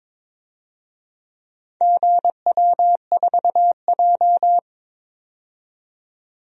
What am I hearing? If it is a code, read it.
Morse code, heard as a keyed tone, about 22 wpm, 700 Hz. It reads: GW4J
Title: DXCC-100 Call Signs - 4 Characters - 22wpm